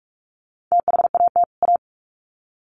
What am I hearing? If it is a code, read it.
Morse code, heard as a keyed tone, about 44 wpm, 700 Hz. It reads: THAT A
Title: Sets of 2 Words 45wpm